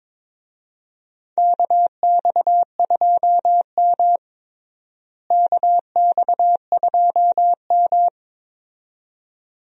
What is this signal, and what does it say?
Morse code, heard as a keyed tone, about 22 wpm, 700 Hz. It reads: KX2M KX2M
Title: USA Call Signs - 4 Characters - Repeated 2x - 22wpm